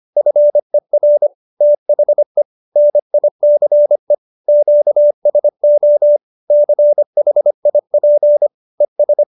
Morse code, heard as a keyed tone, about 25 wpm, 590 Hz. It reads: FER THE NICE QSO CHIP ES